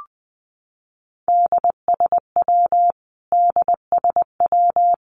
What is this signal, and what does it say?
Morse code, heard as a keyed tone, about 20 wpm, 700 Hz. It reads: DSW DSW
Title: Abbreviations Repeated 3x - 20wpm